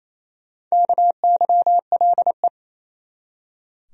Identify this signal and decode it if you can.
Morse code, heard as a keyed tone, about 27 wpm, 700 Hz. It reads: KYLE
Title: USA Common Names - Max 4 Letters - 28wpm